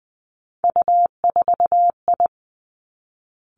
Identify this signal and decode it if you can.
Morse code, heard as a keyed tone, about 20 wpm, 700 Hz. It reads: U4I
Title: Three Challenging Letter-Number Combinations 20wpm